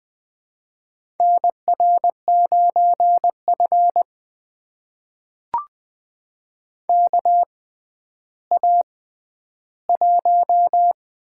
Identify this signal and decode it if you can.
Morse code, heard as a keyed tone, about 20 wpm, spaced as for 11 wpm, 700 Hz. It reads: NR9F KA1
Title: USA Call Signs - 4 Characters - 6x Character Spacing - 20wpm